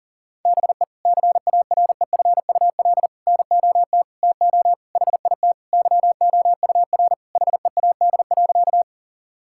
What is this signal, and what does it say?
Morse code, heard as a keyed tone, about 40 wpm, 700 Hz. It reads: BE CAREFUL NOT TO HIT YOUR HEAD.
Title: Sentences from Top 1600 Words 40wpm